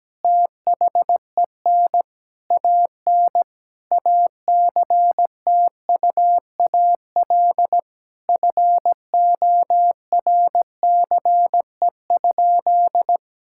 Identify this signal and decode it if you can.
Morse code, heard as a keyed tone, about 17 wpm, 700 Hz. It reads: THEN AN ACTUAL FORCE?
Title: Sentences from Top 1000 Words - Review Part 5 - 17wpm